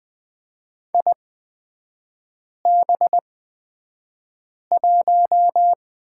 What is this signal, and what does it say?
Morse code, heard as a keyed tone, about 20 wpm, spaced as for 5 wpm, 700 Hz. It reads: IB1
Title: 243 Lesson 39e - Letter z - Three Characters 20wpm